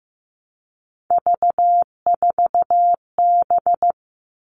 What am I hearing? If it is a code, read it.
Morse code, heard as a keyed tone, about 15 wpm, 700 Hz. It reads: V4B
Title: All Cut Numbers - 3-Digit 15wpm